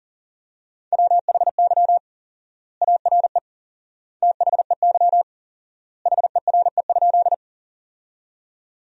Morse code, heard as a keyed tone, about 40 wpm, 700 Hz. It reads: WHY ARE THEY HERE?
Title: 4x Word-Spacing - Sentences from Top 200 Words - 40wpm